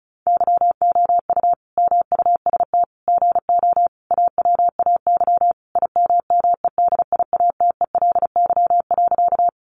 Morse code, heard as a keyed tone, about 34 wpm, 700 Hz. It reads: YOU MUST GO AWAY IMMEDIATELY.
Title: Sentences from Top 1100 Words 35wpm